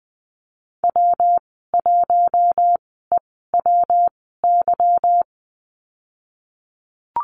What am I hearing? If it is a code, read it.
Morse code, heard as a keyed tone, about 20 wpm, spaced as for 15 wpm, 700 Hz. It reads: W1EWY
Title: 186 Lesson 30g - Letter b - Call Signs 20wpm